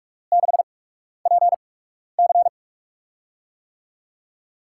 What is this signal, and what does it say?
Morse code, heard as a keyed tone, about 46 wpm, spaced as for 12 wpm, 700 Hz. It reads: 6PC
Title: USA Call Signs - 8x Character Spacing - 45wpm